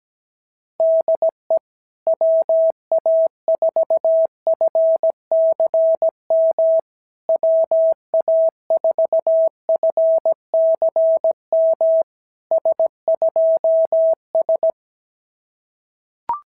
Morse code, heard as a keyed tone, about 17 wpm, 650 Hz. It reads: DE WA4FCM WA4FCM S2S
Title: SOTA QSO Practice 17wpm